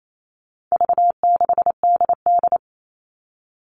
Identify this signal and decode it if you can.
Morse code, heard as a keyed tone, about 28 wpm, 700 Hz. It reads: V6DD
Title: All Cut Numbers - 4-Digit 28wpm